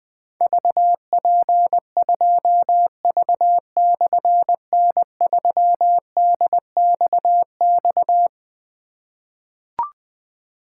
Morse code, heard as a keyed tone, about 20 wpm, 700 Hz. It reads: VP2V/N3DXX
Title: Difficult Call Signs 20wpm